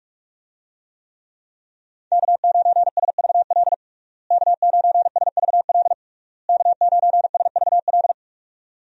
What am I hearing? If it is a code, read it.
Morse code, heard as a keyed tone, about 47 wpm, 700 Hz. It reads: K9SVL K9SVL K9SVL
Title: USA Call Signs - 5 Characters - Repeated 3x - 45wpm